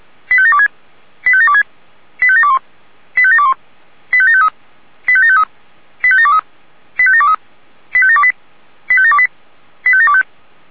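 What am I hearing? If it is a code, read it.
Five-tone selective call (ZVEI), heard as quick runs of five tones, tones of about 70 ms. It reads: 85726 85726 85721 85721 85753 85753 85723 85723 85728 85728 85725
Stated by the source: ZVEI five-tone selective call received by SDR.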